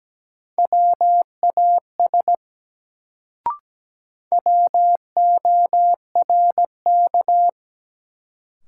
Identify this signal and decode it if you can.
Morse code, heard as a keyed tone, about 17 wpm, 700 Hz. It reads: WAS WORK
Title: Sets of 2 Words - New Word Drill - 17wpm